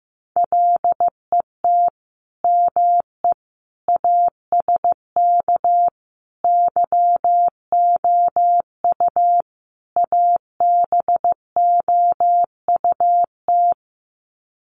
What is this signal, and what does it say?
Morse code, heard as a keyed tone, about 15 wpm, 700 Hz. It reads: LET ME ASK YOU ABOUT
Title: Sets of 5 Words 15wpm